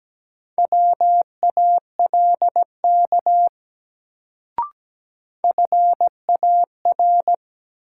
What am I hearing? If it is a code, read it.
Morse code, heard as a keyed tone, about 17 wpm, 700 Hz. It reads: WALK FAR
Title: Sentences from Top 300 Words - New Word Drill - 17wpm